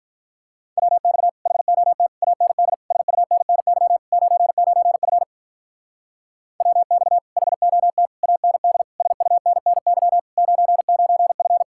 Morse code, heard as a keyed tone, about 43 wpm, 700 Hz. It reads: WX HOT AND SUNNY 99F WX HOT AND SUNNY 99F
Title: Speed-Racing Common QSO Phrases - 35wpm